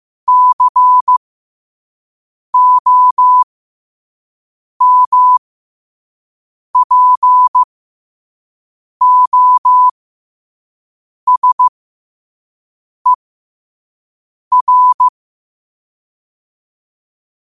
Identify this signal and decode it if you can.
Morse code, heard as a keyed tone, about 15 wpm, spaced as for 5 wpm, 1000 Hz. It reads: COMPOSER
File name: qotdfiles / 5wpm